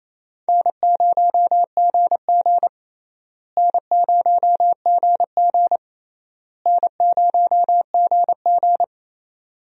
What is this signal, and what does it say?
Morse code, heard as a keyed tone, about 28 wpm, 700 Hz. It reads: N0GG N0GG N0GG
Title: USA Call Signs - 4 Characters - Repeated 3x - 28wpm